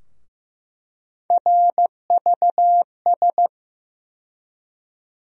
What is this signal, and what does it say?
Morse code, heard as a keyed tone, about 16 wpm, 700 Hz. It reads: RVS
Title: USA Call Sign Postfixes - 3 Letters 15wpm